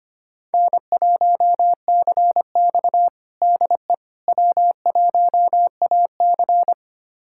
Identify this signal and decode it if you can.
Morse code, heard as a keyed tone, about 25 wpm, 700 Hz. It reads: N1CX DE W1AC
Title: USA Callsign DE Callsign - 4 Characters - 25wpm